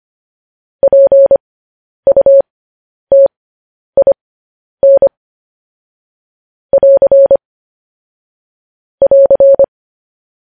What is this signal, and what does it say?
Morse code, heard as a keyed tone, about 25 wpm, spaced as for 10 wpm, 550 Hz. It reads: PUTIN <AR> <AR>
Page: news-headlines-in-morse-code-at-10-wpm